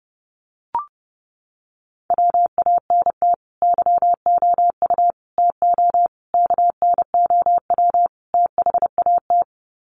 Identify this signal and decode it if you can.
Morse code, heard as a keyed tone, about 30 wpm, 700 Hz. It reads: WANT YOU TO KNOW THAT
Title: Sets of 5 Words Encore 30wpm